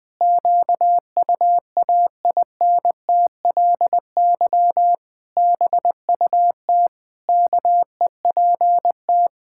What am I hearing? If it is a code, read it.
Morse code, heard as a keyed tone, about 21 wpm, 700 Hz. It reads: QUAINTLY BUT KEPT
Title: Pangram Challenge v2 - 20wpm